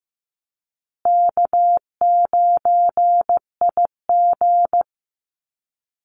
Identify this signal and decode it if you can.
Morse code, heard as a keyed tone, about 14 wpm, 700 Hz. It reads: K9IG
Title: Call Signs Repeated 2x - 15wpm